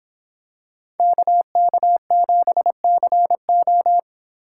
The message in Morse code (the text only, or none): KK7CO